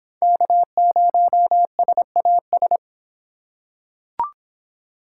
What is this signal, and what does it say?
Morse code, heard as a keyed tone, about 24 wpm, 700 Hz. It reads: K0SAS